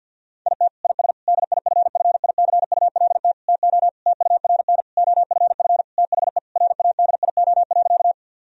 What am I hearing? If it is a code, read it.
Morse code, heard as a keyed tone, about 50 wpm, 700 Hz. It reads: IT IS DIFFICULT TO TURN OFF THE RADIO.
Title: Sentences from Top 1000 Words - Review Part 3 - 50wpm